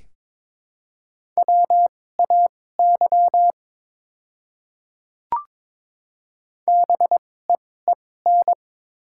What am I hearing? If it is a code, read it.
Morse code, heard as a keyed tone, about 22 wpm, spaced as for 16 wpm, 700 Hz. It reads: WAY BEEN
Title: 2x Character Spacing - Top 100 Words - 22wpm